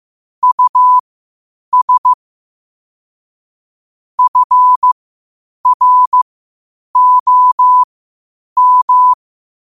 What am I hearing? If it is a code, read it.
Morse code, heard as a keyed tone, about 15 wpm, spaced as for 8 wpm, 1000 Hz. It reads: US FROM
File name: qotdfiles / 8wpm